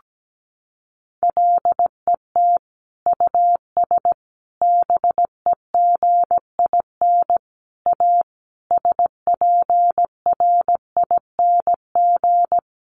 Morse code, heard as a keyed tone, about 17 wpm, 700 Hz. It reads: LET US BEGIN A SPRING
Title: Sentences from Top 1400 Words 17wpm